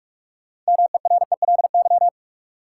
Morse code, heard as a keyed tone, about 46 wpm, 700 Hz. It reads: MERELY
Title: Sentences from Top 1400 Words - New Word Drill Alt - 45wpm